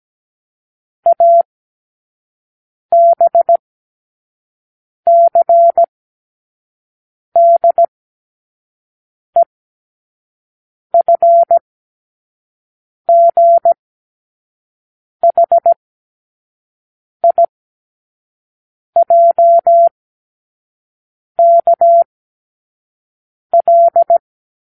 Morse code, heard as a keyed tone, about 17 wpm, spaced as for 5 wpm, 680 Hz. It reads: ABCDEFGHIJKL
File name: k7qo / Lesson18(A-L)